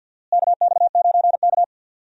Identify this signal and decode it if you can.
Morse code, heard as a keyed tone, about 52 wpm, 700 Hz. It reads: KX9K
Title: USA Call Signs - 4 Characters - Repeated 2x - 50wpm